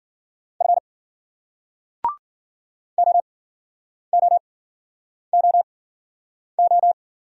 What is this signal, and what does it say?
Morse code, heard as a keyed tone, about 49 wpm, spaced as for 9 wpm, 700 Hz. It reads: F OOOO
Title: Speed-Racing Single Letter-Number 40wpm